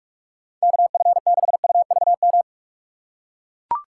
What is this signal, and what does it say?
Morse code, heard as a keyed tone, about 42 wpm, 700 Hz. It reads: KF6UVM
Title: USA Call Signs - 6 Characters - 4x Character Spacing - 45wpm